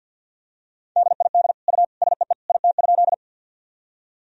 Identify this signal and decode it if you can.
Morse code, heard as a keyed tone, about 46 wpm, 700 Hz. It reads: DID U SEE IT?
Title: Sentences from Top 100 Words and Abbreviations 50wpm